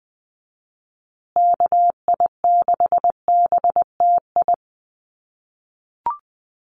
Morse code, heard as a keyed tone, about 20 wpm, 700 Hz. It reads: KI6BTI
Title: USA Call Signs Repeated 2x 20wpm